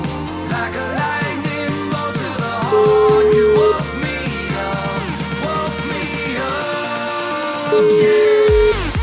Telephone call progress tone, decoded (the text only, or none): ringback tone